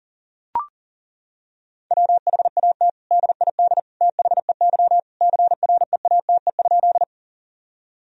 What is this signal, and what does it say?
Morse code, heard as a keyed tone, about 37 wpm, 700 Hz. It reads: WHAT DID THEY CREATE?
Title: Sentences from Top 1000 Words - Review Part 3 - 40wpm